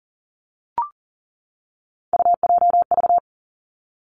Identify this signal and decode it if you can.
Morse code, heard as a keyed tone, about 39 wpm, 700 Hz. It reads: UJV
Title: Three Challenging Letter-Number Combinations 40wpm